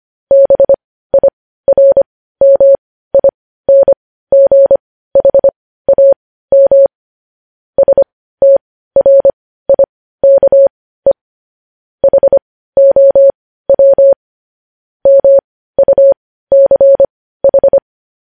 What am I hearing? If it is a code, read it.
Morse code, heard as a keyed tone, about 25 wpm, spaced as for 15 wpm, 550 Hz. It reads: BIRMINGHAM STRIKE HOW MUCH
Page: news headlines in morse code at 15 wpm